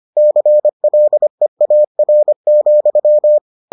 Morse code, heard as a keyed tone, about 25 wpm, 600 Hz. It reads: CLEAR,